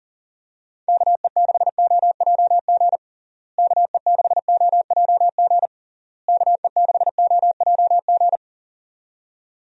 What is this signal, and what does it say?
Morse code, heard as a keyed tone, about 40 wpm, 700 Hz. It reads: KE6OJG KE6OJG KE6OJG
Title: USA Call Signs - 6 Characters - Repeated 3x - 40wpm